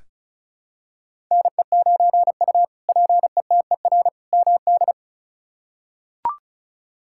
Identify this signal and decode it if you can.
Morse code, heard as a keyed tone, about 36 wpm, 700 Hz. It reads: NE9U PETER MD
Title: K1USN SST QSO Practice - 35wpm